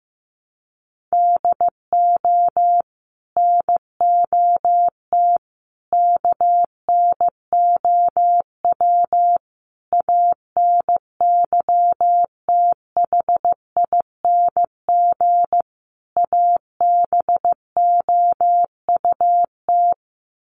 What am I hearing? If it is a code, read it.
Morse code, heard as a keyed tone, about 15 wpm, 700 Hz. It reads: DO NOT KNOW ANYTHING ABOUT
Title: Sets of 5 Words 15wpm